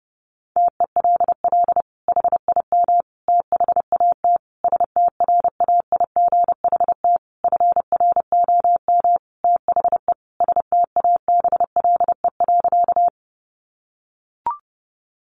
Morse code, heard as a keyed tone, about 28 wpm, 700 Hz. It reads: TELL HIM THAT STRAIGHT FROM THE STABLE.
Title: Sentences from Top 1500 Words 30wpm